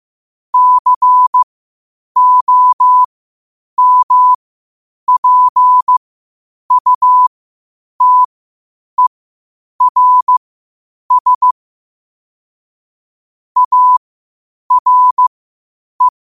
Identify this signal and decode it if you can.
Morse code, heard as a keyed tone, about 15 wpm, spaced as for 8 wpm, 1000 Hz. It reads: COMPUTERS ARE
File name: qotdfiles / 8wpm